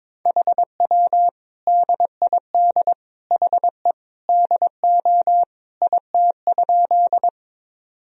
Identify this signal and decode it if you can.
Morse code, heard as a keyed tone, about 22 wpm, 700 Hz. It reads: HW DID HE DO IT?
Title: Sentences from Top 100 Words and Abbreviations 22wpm